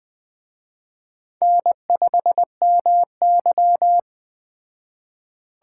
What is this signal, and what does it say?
Morse code, heard as a keyed tone, about 20 wpm, 700 Hz. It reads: N5MY
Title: Speed Racing - USA Call Signs 17wpm